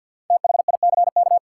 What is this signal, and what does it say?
Morse code, heard as a keyed tone, about 47 wpm, 700 Hz. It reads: THICK
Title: Sentences from Top 1600 Words - New Word Drill Alt - 50wpm